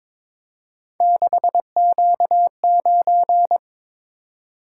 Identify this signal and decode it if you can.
Morse code, heard as a keyed tone, about 22 wpm, 700 Hz. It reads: 6Q9
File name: practice / 22wpm / Three Letter-Number Combinations - 22wpm